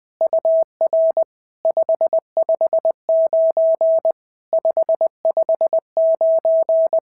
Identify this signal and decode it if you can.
Morse code, heard as a keyed tone, about 20 wpm, 650 Hz. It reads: UR 559 559
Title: POTA QSO Practice - 20wpm